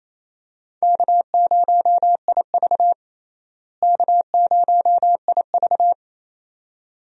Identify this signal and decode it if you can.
Morse code, heard as a keyed tone, about 28 wpm, 700 Hz. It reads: K0IV K0IV
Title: USA Call Signs - 4 Characters - Repeated 2x - 28wpm